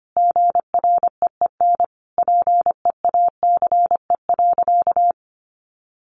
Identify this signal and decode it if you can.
Morse code, heard as a keyed tone, about 25 wpm, 700 Hz. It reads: GREEN PEACE.
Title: A Sound Of Thunder 25wpm